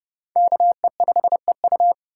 Morse code, heard as a keyed tone, about 29 wpm, 700 Hz. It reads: KE5EU